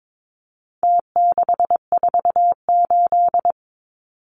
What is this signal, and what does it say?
Morse code, heard as a keyed tone, about 22 wpm, 700 Hz. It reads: T648
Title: Numbers - Contest Style - 4-Digit 22wpm